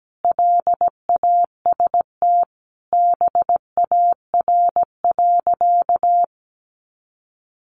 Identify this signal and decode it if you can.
Morse code, heard as a keyed tone, about 17 wpm, 700 Hz. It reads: LAST BAR.
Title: Easy Phrases 17wpm